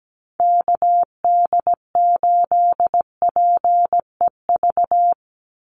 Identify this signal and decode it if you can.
Morse code, heard as a keyed tone, about 17 wpm, 700 Hz. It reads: KD8PEV
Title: USA Call Signs 17wpm